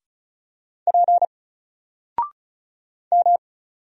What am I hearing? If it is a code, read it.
Morse code, heard as a keyed tone, about 36 wpm, 700 Hz. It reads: P M
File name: Single-Letter-Number-35wpm